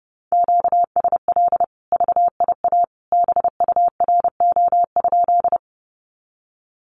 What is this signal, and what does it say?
Morse code, heard as a keyed tone, about 30 wpm, 700 Hz. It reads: QSL VIA BURO?
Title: Common QSO Phrases 30wpm